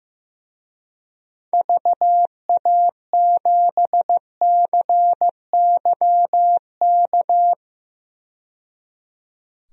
Morse code, heard as a keyed tone, about 15 wpm, 700 Hz. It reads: VA7CYK
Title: DXCC-100 Call Signs - 6 Characters - 15wpm